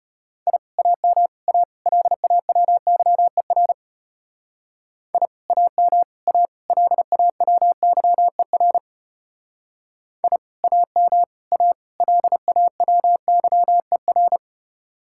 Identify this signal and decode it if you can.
Morse code, heard as a keyed tone, about 32 wpm, 700 Hz. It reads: I AM A LAWYER I AM A LAWYER I AM A LAWYER